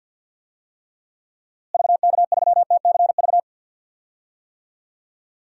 Morse code, heard as a keyed tone, about 51 wpm, 700 Hz. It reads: VK3TCV
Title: DXCC-100 Call Signs - 6 Characters - Repeated 2x - 50wpm